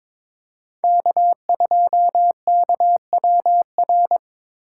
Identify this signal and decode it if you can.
Morse code, heard as a keyed tone, about 26 wpm, 700 Hz. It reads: K2KWR
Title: Speed Racing - USA 1x3 Call Signs 22wpm